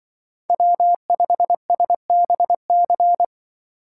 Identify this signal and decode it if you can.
Morse code, heard as a keyed tone, about 25 wpm, 700 Hz. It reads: W5SBC
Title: Speed Racing - USA 1x3 Call Signs 20wpm